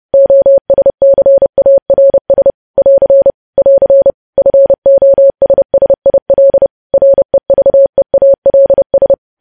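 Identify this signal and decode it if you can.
Morse code, heard as a keyed tone, about 30 wpm, 550 Hz. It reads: OSCARS <AR> <AR> FOSSIL REVEALS
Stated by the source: News headlines sent at 30 wpm.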